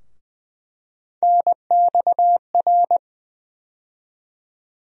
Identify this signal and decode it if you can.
Morse code, heard as a keyed tone, about 19 wpm, 700 Hz. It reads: NXR
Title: USA Call Sign Postfixes - 3 Letters 20wpm